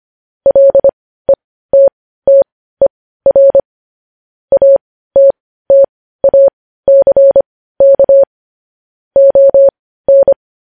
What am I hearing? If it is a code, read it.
Morse code, heard as a keyed tone, about 25 wpm, spaced as for 15 wpm, 550 Hz. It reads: LETTER ATTACK ON
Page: News Headlines in Morse Code at 15 WPM